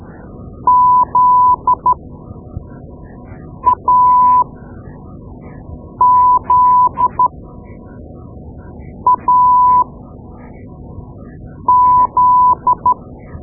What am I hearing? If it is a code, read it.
Morse code, heard as a keyed tone, about 13 wpm, spaced as for 4 wpm, 1000 Hz. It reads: ZAZAZ
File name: Morse Lessons / Morse Lessons / Tape-2B